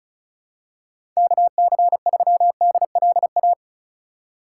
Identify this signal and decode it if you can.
Morse code, heard as a keyed tone, about 34 wpm, 700 Hz. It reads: KC3DLA
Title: USA Call Signs - 6 Characters - 2x Character Spacing - 35wpm